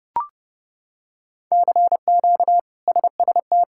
Morse code, heard as a keyed tone, about 30 wpm, 700 Hz. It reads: CQ SST
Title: K1USN SST QSO Practice - 30wpm